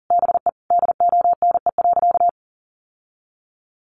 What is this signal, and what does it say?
Morse code, heard as a keyed tone, about 37 wpm, 700 Hz. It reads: BE DONE.
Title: Sentences from Top 1100 Words 40wpm